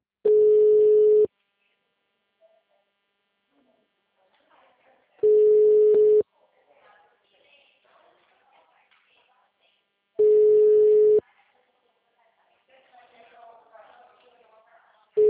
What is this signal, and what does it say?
Ringback tone (Europe, 425 Hz), with speech.